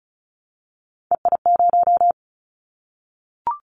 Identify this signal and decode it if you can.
Morse code, heard as a keyed tone, about 35 wpm, 690 Hz. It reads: EI0
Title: Callsign Prefixes for Top 50 DXCC - 35wpm